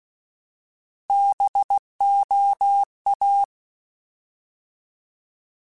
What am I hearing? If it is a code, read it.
Morse code, heard as a keyed tone, about 16 wpm, 800 Hz. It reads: BOA
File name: Alphabetical / Alphabetical mp3s / Bs @15